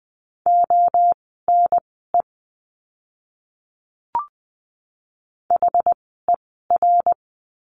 Morse code, heard as a keyed tone, about 20 wpm, spaced as for 14 wpm, 700 Hz. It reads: ONE HER
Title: 061 Lesson 11f - Letter h - Words 20wpm